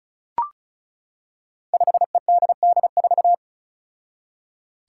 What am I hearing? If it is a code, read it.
Morse code, heard as a keyed tone, about 36 wpm, 700 Hz. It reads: 5EDD4
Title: Five Challenging Letter-Number Combinations 35wpm